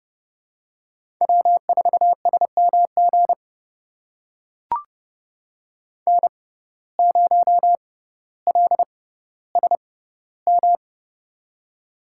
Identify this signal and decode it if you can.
Morse code, heard as a keyed tone, about 30 wpm, spaced as for 15 wpm, 700 Hz. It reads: W4SMG N0LSM
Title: USA Call Signs - 5 Characters - 6x Character Spacing - 30wpm